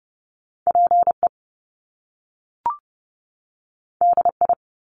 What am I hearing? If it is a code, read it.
Morse code, heard as a keyed tone, about 30 wpm, 700 Hz. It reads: PE DI
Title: Top Two Letter Combinations in Words 30wpm